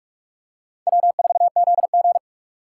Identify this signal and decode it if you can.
Morse code, heard as a keyed tone, about 44 wpm, 700 Hz. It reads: W4ZG
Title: USA Call Signs - 4 Characters - 2x Character Spacing - 45wpm